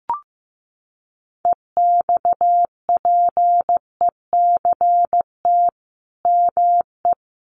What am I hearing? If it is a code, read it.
Morse code, heard as a keyed tone, about 15 wpm, 700 Hz. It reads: EXPECT ME
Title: Sentences from Top 1100 Words 15wpm